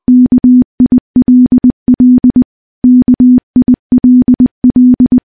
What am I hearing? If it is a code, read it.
Morse code, heard as a keyed tone, about 20 wpm, 250 Hz. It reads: KILL KILL
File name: morse-training-20WPM